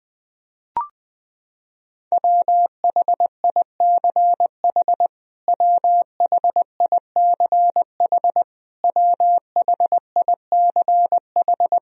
Morse code, heard as a keyed tone, about 20 wpm, 700 Hz. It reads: WHICH WHICH WHICH
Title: Top 100 Words Repeated 20wpm